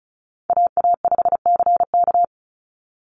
Morse code, heard as a keyed tone, about 36 wpm, 700 Hz. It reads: AA5CK
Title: NA SOTA Chaser Callsigns 35wpm